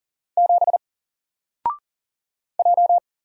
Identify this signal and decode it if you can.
Morse code, heard as a keyed tone, about 40 wpm, 700 Hz. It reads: 7 J
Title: Single Letter-Number Alt 40wpm